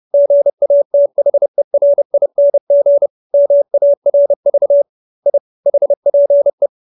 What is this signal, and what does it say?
Morse code, heard as a keyed tone, about 30 wpm, 570 Hz. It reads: GATHERING MARV I HPE